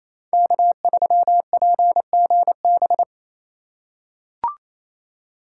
Morse code, heard as a keyed tone, about 28 wpm, 690 Hz. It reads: K3PGB
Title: USA Call Signs - 2x Character Spacing - 28wpm